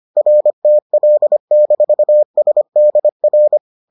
Morse code, heard as a keyed tone, about 25 wpm, 600 Hz. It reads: RTL-SDR